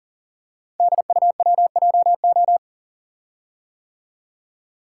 Morse code, heard as a keyed tone, about 42 wpm, 690 Hz. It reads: DUWJO
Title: Code Groups - 5 Letter:Numbers - 40wpm